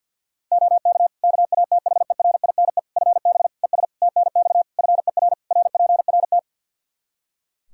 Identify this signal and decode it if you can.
Morse code, heard as a keyed tone, about 52 wpm, 700 Hz. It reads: OK KATHERINE FB ES TNX FER RPRT
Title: K1USN SST QSO Practice 50wpm